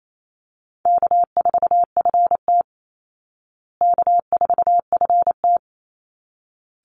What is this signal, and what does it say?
Morse code, heard as a keyed tone, about 28 wpm, 700 Hz. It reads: K4FT K4FT
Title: Easy Call Signs Repeated 2x - 28wpm